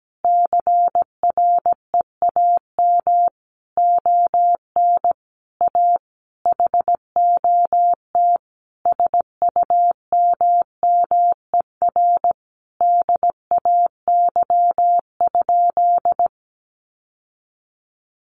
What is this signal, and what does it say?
Morse code, heard as a keyed tone, about 17 wpm, 700 Hz. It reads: CREAM ON A HOT SUMMER DAY?
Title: Sentences from Top 1500 Words 17wpm